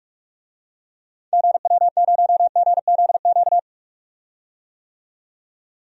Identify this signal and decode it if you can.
Morse code, heard as a keyed tone, about 42 wpm, 700 Hz. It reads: GW0GZQ
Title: DXCC-100 Call Signs - 6 Characters - 45wpm